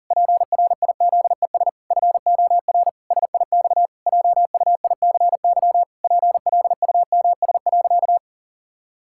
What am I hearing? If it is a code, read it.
Morse code, heard as a keyed tone, about 40 wpm, 700 Hz. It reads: PRIZES FOR SIX JUICY PLUMS.
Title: Pangram Challenge v2 - 40wpm